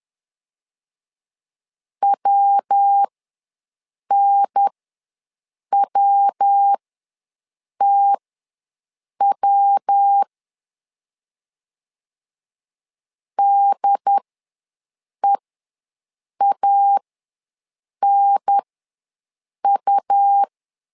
Morse code, heard as a keyed tone, about 10 wpm, spaced as for 6 wpm, 800 Hz. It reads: WNWTW DEANU
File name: LEZIONI / 3 MWRO / 3 MW ws 10wpm 2213